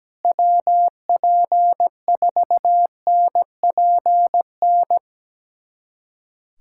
Morse code, heard as a keyed tone, about 17 wpm, 700 Hz. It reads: WP4NPN
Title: USA Callsign DE Callsign 17wpm